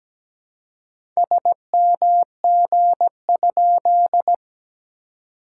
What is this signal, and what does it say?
Morse code, heard as a keyed tone, about 17 wpm, 690 Hz. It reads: SMG?
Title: SOTA QSO Practice 17wpm